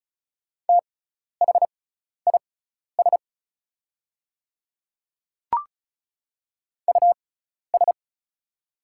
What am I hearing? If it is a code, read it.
Morse code, heard as a keyed tone, about 34 wpm, spaced as for 12 wpm, 700 Hz. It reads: THIS US
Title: 6x Character Spacing - Top 200 Words - 35wpm